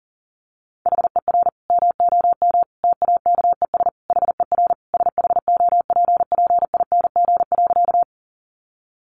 Morse code, heard as a keyed tone, about 40 wpm, 700 Hz. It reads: HER MOM TAKES HER SHOPPING.
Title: Additional Easy Phrases 40wpm